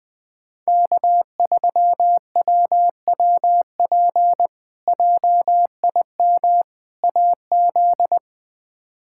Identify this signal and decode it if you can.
Morse code, heard as a keyed tone, about 21 wpm, 700 Hz. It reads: K3WWP JIM AZ
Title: K1USN SST QSO Practice - 20wpm